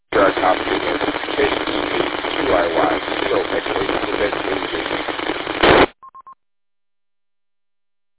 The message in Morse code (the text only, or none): none